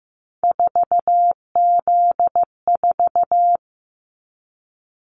Morse code, heard as a keyed tone, about 15 wpm, 700 Hz. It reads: 4Z4
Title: Callsign Prefixes for Top 50 DXCC - 15wpm